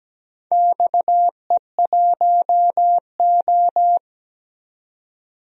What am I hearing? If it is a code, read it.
Morse code, heard as a keyed tone, about 17 wpm, 700 Hz. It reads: XE1O
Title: DXCC-100 Call Signs - 4 Characters - 17wpm